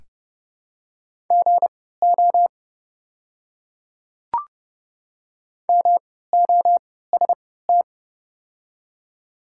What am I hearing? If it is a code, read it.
Morse code, heard as a keyed tone, about 30 wpm, spaced as for 17 wpm, 700 Hz. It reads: GO MOST
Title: 3x Character Spacing - Top 100 Words - 30wpm